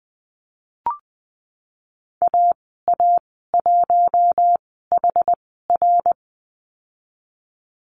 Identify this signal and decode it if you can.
Morse code, heard as a keyed tone, about 20 wpm, spaced as for 15 wpm, 700 Hz. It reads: AA1HR
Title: 076 Lesson 13g - Letter L - Call Signs 20wpm